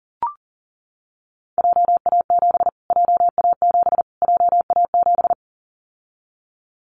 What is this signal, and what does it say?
Morse code, heard as a keyed tone, about 40 wpm, 700 Hz. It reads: JA7 JA7 JA7
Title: Callsign Prefixes for Top 25 DXCC - Repeated 3x - 40wpm